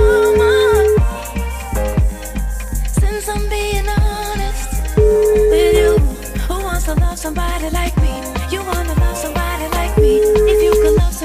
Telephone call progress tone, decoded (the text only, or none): ringback tone